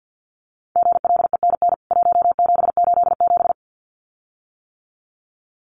Morse code, heard as a keyed tone, about 48 wpm, 700 Hz. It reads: GLENN 1787
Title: CWT Exchange 50wpm